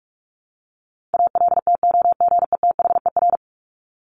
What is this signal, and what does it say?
Morse code, heard as a keyed tone, about 44 wpm, 700 Hz. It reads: ALTOGETHER
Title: Top 1501-1750 Words 45wpm